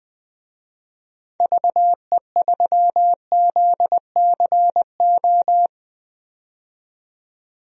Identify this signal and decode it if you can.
Morse code, heard as a keyed tone, about 20 wpm, 700 Hz. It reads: VE3ZCO
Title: DXCC-100 Call Signs - 6 Characters - 20wpm